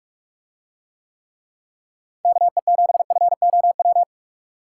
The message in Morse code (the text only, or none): KE7FOW